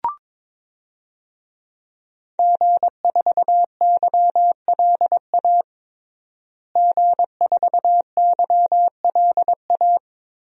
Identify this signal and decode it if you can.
Morse code, heard as a keyed tone, about 22 wpm, 700 Hz. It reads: G4YLA G4YLA
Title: DXCC-100 Call Signs - 5 Characters - Repeated 3x - 22wpm